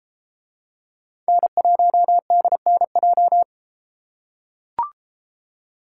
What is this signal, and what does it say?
Morse code, heard as a keyed tone, about 32 wpm, 700 Hz. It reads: N1DNJ